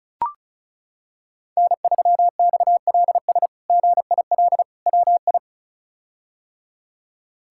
Morse code, heard as a keyed tone, about 36 wpm, 700 Hz. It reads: N3XLS GIL WI
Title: K1USN SST QSO Practice - 35wpm